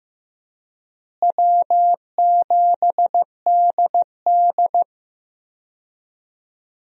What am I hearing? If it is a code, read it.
Morse code, heard as a keyed tone, about 15 wpm, 700 Hz. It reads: W7DD
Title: USA Call Signs - 4 Characters - Repeated 3x - 15wpm